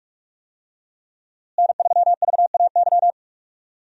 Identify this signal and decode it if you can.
Morse code, heard as a keyed tone, about 41 wpm, 700 Hz. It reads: N3VAY
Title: USA Call Signs - 5 Characters - Repeated 2x - 45wpm